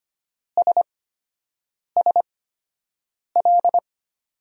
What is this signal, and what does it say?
Morse code, heard as a keyed tone, about 25 wpm, spaced as for 7 wpm, 700 Hz. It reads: SSL
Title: USA Call Signs - 5 Characters - 8x Character Spacing - 25wpm